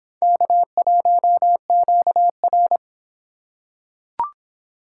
Morse code, heard as a keyed tone, about 24 wpm, 700 Hz. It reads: K1QR